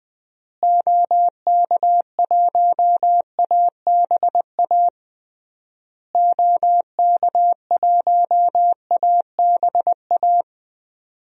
Morse code, heard as a keyed tone, about 21 wpm, 700 Hz. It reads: OK1ABA OK1ABA
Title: DXCC-100 Call Signs - 6 Characters - Repeated 3x - 20wpm